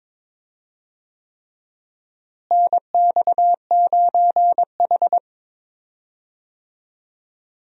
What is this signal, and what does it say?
Morse code, heard as a keyed tone, about 21 wpm, 700 Hz. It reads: NX9H
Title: USA Call Signs - 4 Characters - Repeated 3x - 22wpm